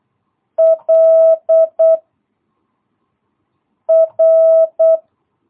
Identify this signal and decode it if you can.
Morse code, heard as a keyed tone, about 8 wpm, 660 Hz. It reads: L R
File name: jh8cbh / 2022cw